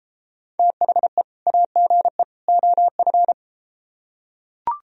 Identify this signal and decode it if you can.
Morse code, heard as a keyed tone, about 33 wpm, 700 Hz. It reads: THE AGE OF